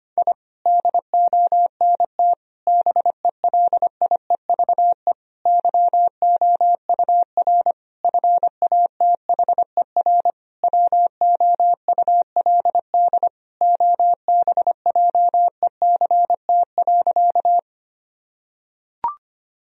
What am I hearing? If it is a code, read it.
Morse code, heard as a keyed tone, about 25 wpm, 700 Hz. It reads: I DONT BELIEVE YOUR FATHER WOULD OBJECT.
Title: Sentences from Top 1000 Words - Review Part 3 - 25wpm